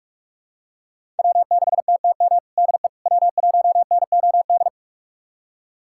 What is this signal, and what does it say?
Morse code, heard as a keyed tone, about 42 wpm, 700 Hz. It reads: W6TTM DE W1NOD
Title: USA Callsign DE Callsign - 5 Characters - 45wpm